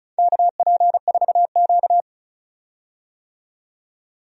Morse code, heard as a keyed tone, about 36 wpm, 700 Hz. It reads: KP4Q
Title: DXCC-100 Call Signs - 4 Characters - 35wpm